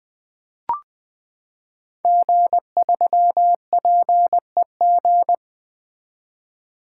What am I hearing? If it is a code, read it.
Morse code, heard as a keyed tone, about 20 wpm, 700 Hz. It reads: G3PEG
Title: Code Groups - 5 Letter:Numbers - 20wpm